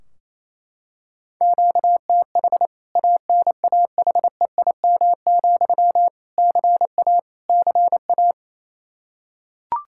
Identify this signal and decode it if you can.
Morse code, heard as a keyed tone, about 27 wpm, 700 Hz. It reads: QTH ANAHEIM, CA CA
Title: K1USN SST QSO Practice 28wpm